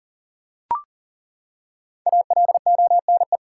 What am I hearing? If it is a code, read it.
Morse code, heard as a keyed tone, about 40 wpm, 700 Hz. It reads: ALONE
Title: Five Letter Words - 40wpm